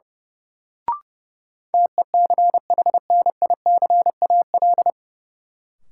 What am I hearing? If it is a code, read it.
Morse code, heard as a keyed tone, about 32 wpm, 700 Hz. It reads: TECHNICAL
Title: Sentences from Top 1100 Words - New Word Drill - 30wpm